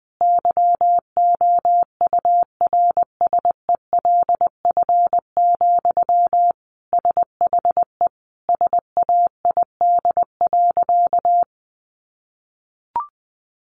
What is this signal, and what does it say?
Morse code, heard as a keyed tone, about 20 wpm, 700 Hz. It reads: YOURSELF, SHE SAID.
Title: Sentences from Top 600 Words 20wpm